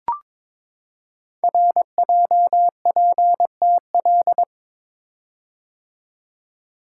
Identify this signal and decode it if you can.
Morse code, heard as a keyed tone, about 22 wpm, 700 Hz. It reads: RJPTL
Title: Code Groups - 5 Letters - 22wpm